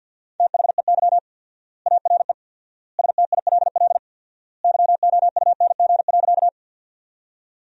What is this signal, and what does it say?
Morse code, heard as a keyed tone, about 50 wpm, 700 Hz. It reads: THEY ARE STILL YOUNG.
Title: 4x Word-Spacing - Sentences from Top 400 Words - 50wpm